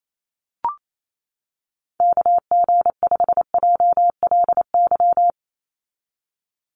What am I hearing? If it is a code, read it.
Morse code, heard as a keyed tone, about 28 wpm, 700 Hz. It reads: KG5JLY
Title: USA Call Signs 28wpm